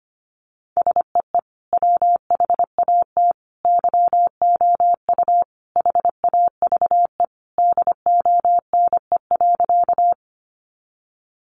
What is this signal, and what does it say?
Morse code, heard as a keyed tone, about 25 wpm, 700 Hz. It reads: SEE WHAT YOU HAVE DONE.
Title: Sentences from Top 500 Words 25wpm